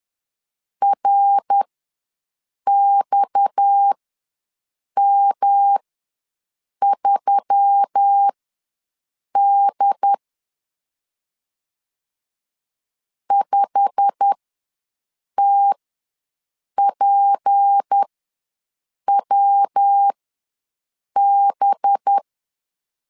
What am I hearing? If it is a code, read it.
Morse code, heard as a keyed tone, about 10 wpm, spaced as for 6 wpm, 800 Hz. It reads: RXM3D 5TPWB
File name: LEZIONI / 9 2 ws 10wpm 4130